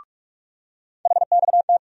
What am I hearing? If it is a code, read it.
Morse code, heard as a keyed tone, about 44 wpm, 700 Hz. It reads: HXT
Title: USA Call Sign Postfixes - 3 Letters 45wpm